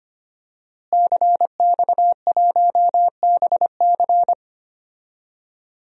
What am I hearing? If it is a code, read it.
Morse code, heard as a keyed tone, about 25 wpm, 700 Hz. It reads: CX1BC
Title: DXCC-100 Call Signs - 5 Characters - Repeated 2x - 25wpm